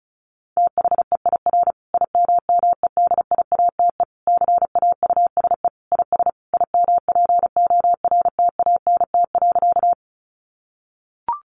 Morse code, heard as a keyed tone, about 34 wpm, 700 Hz. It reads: THEIR IMMEDIATE CAUSE IS IMPORTANT.
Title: Sentences from Top 1200 Words 35wpm